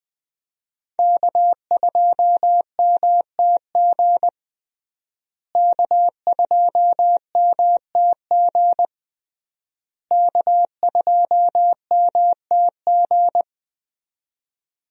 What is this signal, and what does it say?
Morse code, heard as a keyed tone, about 21 wpm, 700 Hz. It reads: K2MTG K2MTG K2MTG
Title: USA Call Signs - 5 Characters - Repeated 3x - 20wpm